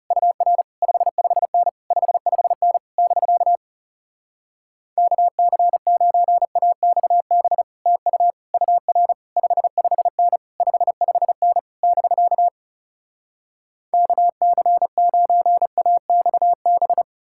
Morse code, heard as a keyed tone, about 40 wpm, 700 Hz. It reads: UR 55N 55N <BK> KC9AXB TU UR 55N 55N <BK> KC9AXB